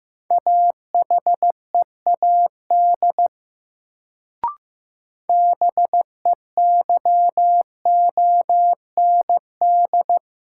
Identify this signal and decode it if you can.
Morse code, heard as a keyed tone, about 15 wpm, 700 Hz. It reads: AHEAD BEYOND
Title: Sentences from Top 1100 Words - New Word Drill Alt - 15wpm